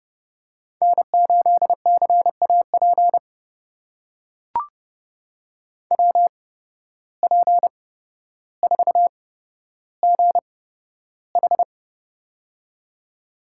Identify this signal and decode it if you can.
Morse code, heard as a keyed tone, about 31 wpm, spaced as for 13 wpm, 700 Hz. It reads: N8CAP WP4GH
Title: USA Call Signs - 5 Characters - 8x Character Spacing - 30wpm